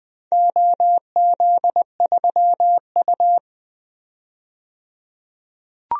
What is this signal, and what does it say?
Morse code, heard as a keyed tone, about 20 wpm, 700 Hz. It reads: OZ3U
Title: DXCC-100 Call Signs - 4 Characters - 20wpm